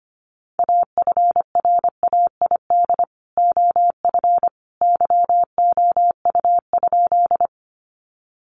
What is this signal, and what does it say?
Morse code, heard as a keyed tone, about 25 wpm, 700 Hz. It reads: AFRAID OF YOU?
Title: Sentences from Top 1500 Words 25wpm